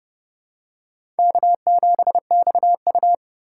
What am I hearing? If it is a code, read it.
Morse code, heard as a keyed tone, about 30 wpm, 700 Hz. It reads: K7XU
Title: USA Call Signs - 4 Characters - 3x Character Spacing - 30wpm